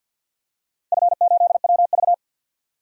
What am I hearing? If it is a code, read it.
Morse code, heard as a keyed tone, about 46 wpm, 700 Hz. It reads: F8WV
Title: DXCC-100 Call Signs - 4 Characters - Repeated 3x - 50wpm